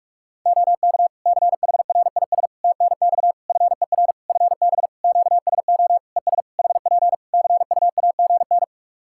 Morse code, heard as a keyed tone, about 47 wpm, 700 Hz. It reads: OK CHRIS TNX FER FB QSO ES HP CUAGN
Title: Practice QSO - Exchange 4 - Ending - 45wpm